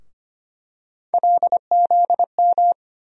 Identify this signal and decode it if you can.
Morse code, heard as a keyed tone, about 25 wpm, 700 Hz. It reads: LZM
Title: USA Call Sign Postfixes - 3 Letters 25wpm